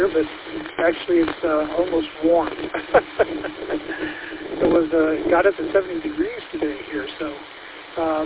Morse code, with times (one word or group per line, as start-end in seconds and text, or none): none